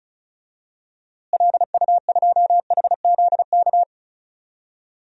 Morse code, heard as a keyed tone, about 36 wpm, 700 Hz. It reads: LU2HZK
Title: DXCC-100 Call Signs - 6 Characters - Repeated 3x - 35wpm